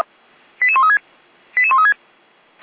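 Five-tone selective call (ZVEI), heard as quick runs of five tones, tones of about 70 ms. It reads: 88136 88136
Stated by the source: Five-tone ZVEI selective call received by SDR.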